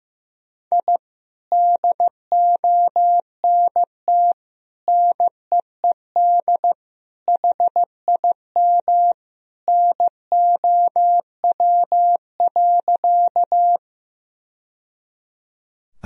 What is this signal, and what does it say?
Morse code, heard as a keyed tone, about 15 wpm, 700 Hz. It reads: I DONT NEED HIM NOW.
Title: Sentences from Top 1000 Words - Review Part 1 - 15wpm